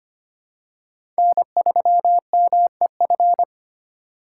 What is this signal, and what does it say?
Morse code, heard as a keyed tone, about 25 wpm, 700 Hz. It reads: N3MEF
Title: USA Call Signs - 5 Characters - 3x Character Spacing - 25wpm